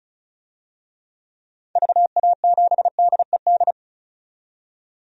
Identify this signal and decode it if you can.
Morse code, heard as a keyed tone, about 36 wpm, 690 Hz. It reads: VA7DED
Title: DXCC-100 Call Signs - 6 Characters - 35wpm